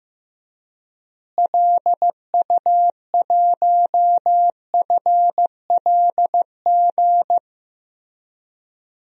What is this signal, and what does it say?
Morse code, heard as a keyed tone, about 15 wpm, 700 Hz. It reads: LU1FLG
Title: DXCC-100 Call Signs - 6 Characters - Repeated 3x - 15wpm